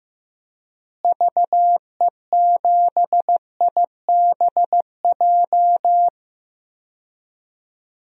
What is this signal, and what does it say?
Morse code, heard as a keyed tone, about 15 wpm, 700 Hz. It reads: VE7IBJ
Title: DXCC-100 Call Signs - 6 Characters - Repeated 3x - 15wpm